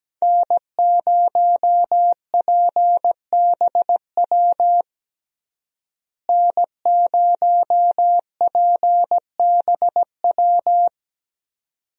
Morse code, heard as a keyed tone, about 17 wpm, 700 Hz. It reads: N0PBW N0PBW
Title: USA Call Signs - 5 Characters - Repeated 2x - 17wpm